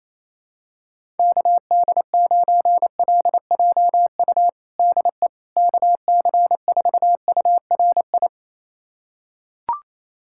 Morse code, heard as a keyed tone, about 28 wpm, 700 Hz. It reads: KD9LJU DE KC4URI
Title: USA Callsign DE Callsign 28wpm